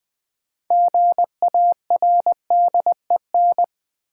Morse code, heard as a keyed tone, about 20 wpm, 700 Hz. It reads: GARDEN
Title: Sentences from Top 700 Words - New Word Drill - 20wpm